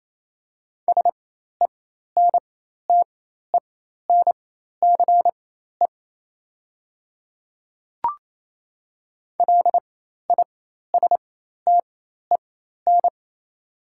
Morse code, heard as a keyed tone, about 28 wpm, spaced as for 13 wpm, 700 Hz. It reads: SENTENCE LISTEN
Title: 4x Character Spacing - Top 500 Words - 28wpm